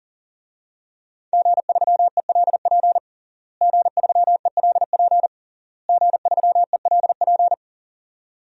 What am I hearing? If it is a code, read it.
Morse code, heard as a keyed tone, about 40 wpm, 700 Hz. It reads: G3ELP G3ELP G3ELP
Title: DXCC-100 Call Signs - 5 Characters - Repeated 3x - 40wpm